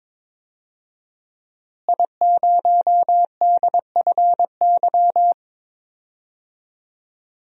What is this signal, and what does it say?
Morse code, heard as a keyed tone, about 22 wpm, 700 Hz. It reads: I0DFY
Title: DXCC-100 Call Signs - 5 Characters - Repeated 3x - 22wpm